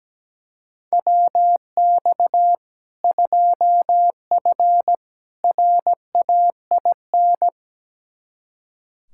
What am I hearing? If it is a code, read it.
Morse code, heard as a keyed tone, about 17 wpm, 700 Hz. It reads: WX 2F RAIN
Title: K1USN SST QSO Practice 17wpm